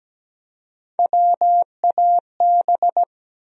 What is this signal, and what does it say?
Morse code, heard as a keyed tone, about 17 wpm, 700 Hz. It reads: WAB
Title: USA Call Sign Postfixes - 3 Letters 17wpm